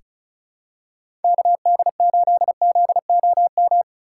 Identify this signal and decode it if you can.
Morse code, heard as a keyed tone, about 36 wpm, 700 Hz. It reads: KD8ZOM
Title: USA Call Signs - 3x Character Spacing - 35wpm